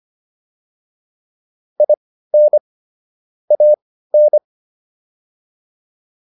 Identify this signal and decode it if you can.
Morse code, heard as a keyed tone, about 25 wpm, spaced as for 15 wpm, 600 Hz. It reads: IN AN